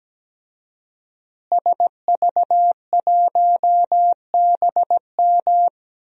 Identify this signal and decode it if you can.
Morse code, heard as a keyed tone, about 17 wpm, 700 Hz. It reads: SV1BM
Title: DXCC-100 Call Signs - 5 Characters - Repeated 3x - 17wpm